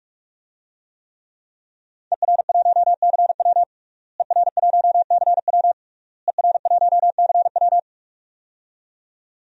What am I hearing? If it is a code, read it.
Morse code, heard as a keyed tone, about 46 wpm, 700 Hz. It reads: ER1CW ER1CW ER1CW
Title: DXCC-100 Call Signs - 5 Characters - Repeated 3x - 45wpm